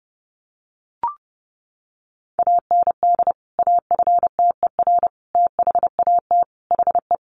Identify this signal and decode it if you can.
Morse code, heard as a keyed tone, about 30 wpm, 700 Hz. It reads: AND AFTER THAT HE
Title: Sentences from Top 1400 Words 30wpm